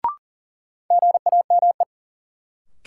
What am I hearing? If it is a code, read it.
Morse code, heard as a keyed tone, about 39 wpm, 700 Hz. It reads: GAME
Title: Sentences from Top 500 Words - New Word Drill - 40wpm